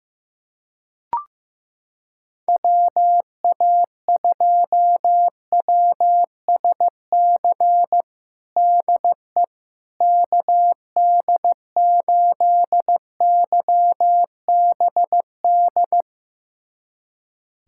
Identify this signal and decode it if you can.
Morse code, heard as a keyed tone, about 15 wpm, 700 Hz. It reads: WA2WSC DE KD8YBD
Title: USA Callsign DE Callsign - 6 Characters - 15wpm